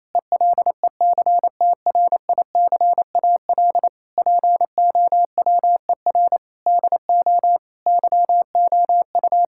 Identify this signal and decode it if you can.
Morse code, heard as a keyed tone, about 28 wpm, 700 Hz. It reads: ELECTRICAL POWER DO YOU
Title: Sentences from Top 1000 Words - Review Part 6 - 28wpm